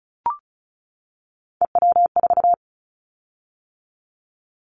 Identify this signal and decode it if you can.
Morse code, heard as a keyed tone, about 36 wpm, 700 Hz. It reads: EW4
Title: Callsign Prefixes for Top 50 DXCC - 35wpm